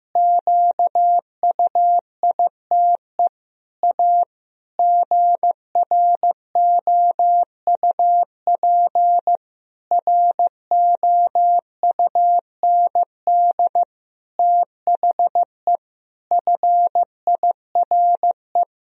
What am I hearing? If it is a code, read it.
Morse code, heard as a keyed tone, about 16 wpm, 700 Hz. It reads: QUITE A GROUP ROUND THE FIRE
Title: Sentences from Top 1000 Words - Review Part 1 - 15wpm